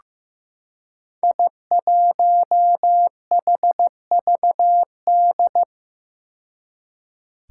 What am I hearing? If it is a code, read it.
Morse code, heard as a keyed tone, about 15 wpm, 700 Hz. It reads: I1HVD
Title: Five Challenging Letter-Number Combinations 15wpm